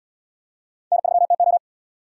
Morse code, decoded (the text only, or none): N3TO